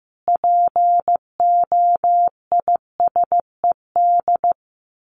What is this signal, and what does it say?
Morse code, heard as a keyed tone, about 15 wpm, 700 Hz. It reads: POISED
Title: A Sound Of Thunder 15wpm